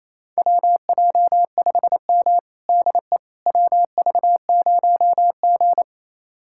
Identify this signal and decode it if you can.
Morse code, heard as a keyed tone, about 28 wpm, 700 Hz. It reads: WJ5M DE WV0G
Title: USA Callsign DE Callsign - 4 Characters - 28wpm